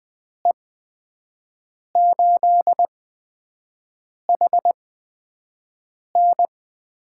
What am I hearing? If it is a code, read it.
Morse code, heard as a keyed tone, about 20 wpm, spaced as for 5 wpm, 700 Hz. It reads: E8HN
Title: USA Call Signs - 5 Characters - 8x Character Spacing - 20wpm